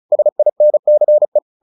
Morse code, heard as a keyed tone, about 34 wpm, 600 Hz. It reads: SINCE
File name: the old man and the sea 2 35wpm